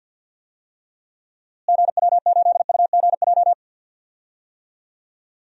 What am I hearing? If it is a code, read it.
Morse code, heard as a keyed tone, about 49 wpm, 700 Hz. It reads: GW8UGJ
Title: DXCC-100 Call Signs - 6 Characters - Repeated 3x - 50wpm